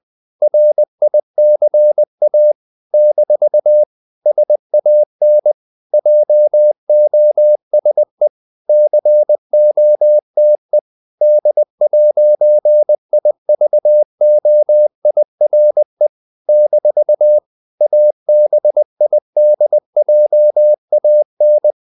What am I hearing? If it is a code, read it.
Morse code, heard as a keyed tone, about 20 wpm, 600 Hz. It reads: RICA - SAN JOSE COTE D'IVOIRE - ABIDJAN
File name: CW / Practice / NationalCapitals20